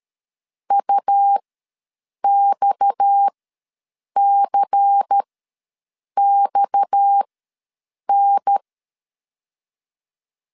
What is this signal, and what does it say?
Morse code, heard as a keyed tone, about 13 wpm, spaced as for 7 wpm, 800 Hz. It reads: UXCXN
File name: LEZIONI / 6 PKX0 / 6 PKX ws 12wpm 3106